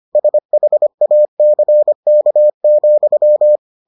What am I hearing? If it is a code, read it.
Morse code, heard as a keyed tone, about 25 wpm, 600 Hz. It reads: SHACK,